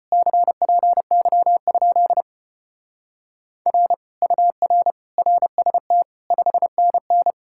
Morse code, heard as a keyed tone, about 29 wpm, 700 Hz. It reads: CPY? R UR RST 5NN
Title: Speed-Racing Common QSO Phrases - 25wpm